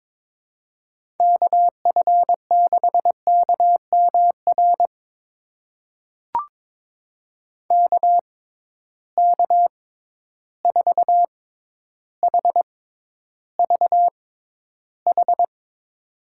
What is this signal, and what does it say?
Morse code, heard as a keyed tone, about 22 wpm, spaced as for 11 wpm, 700 Hz. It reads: KF6KMR KK4HVH
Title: USA Call Signs - 6 Characters - 6x Character Spacing - 22wpm